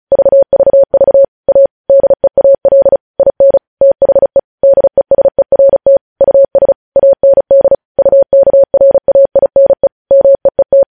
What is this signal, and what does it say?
Morse code, heard as a keyed tone, about 36 wpm, 550 Hz. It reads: VVV A DEAL IN THE DESERT US AND UKRAINE MEET
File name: podcast / News headlines-1x-35x35